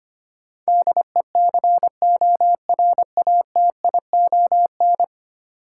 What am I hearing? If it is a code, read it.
Morse code, heard as a keyed tone, about 25 wpm, 690 Hz. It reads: DECORATION
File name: 10 Letter Words - First 3 Letters Given - 25wpm